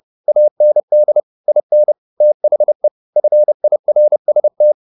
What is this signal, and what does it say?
Morse code, heard as a keyed tone, about 30 wpm, 600 Hz. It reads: AND IN THE FIRST
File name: the old man and the sea / the old man and the sea 1 30wpm